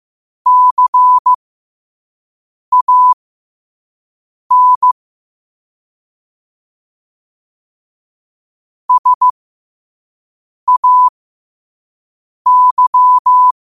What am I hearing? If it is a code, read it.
Morse code, heard as a keyed tone, about 15 wpm, spaced as for 5 wpm, 1000 Hz. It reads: CAN SAY
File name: qotdfiles / 5wpm